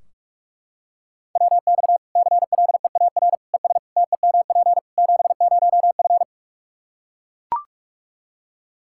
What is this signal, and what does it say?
Morse code, heard as a keyed tone, about 44 wpm, 700 Hz. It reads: WX CLEAR ES TEMP 70F
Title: Practice QSO - Exchange 2 - Info - 45wpm